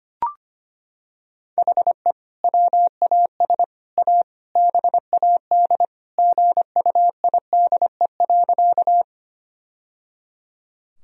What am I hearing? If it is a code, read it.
Morse code, heard as a keyed tone, about 25 wpm, 700 Hz. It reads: HE WAS A BAD GUIDE.
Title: Sentences from Top 1000 Words - Review Part 3 - 25wpm